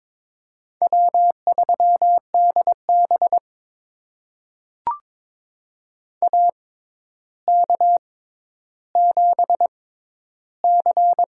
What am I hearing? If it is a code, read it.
Morse code, heard as a keyed tone, about 22 wpm, spaced as for 11 wpm, 700 Hz. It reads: W3DB AK7C
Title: USA Call Signs - 4 Characters - 6x Character Spacing - 22wpm